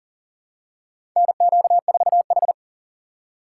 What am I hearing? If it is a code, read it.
Morse code, heard as a keyed tone, about 40 wpm, 700 Hz. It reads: NQ4H
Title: USA Call Signs - 4 Characters - 3x Character Spacing - 40wpm